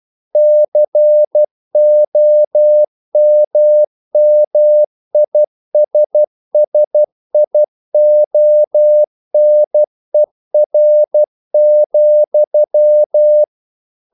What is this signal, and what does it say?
Morse code, heard as a keyed tone, about 12 wpm, 600 Hz. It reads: COMMISSIONER,
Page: English quotations in CW (telegraphy - morse)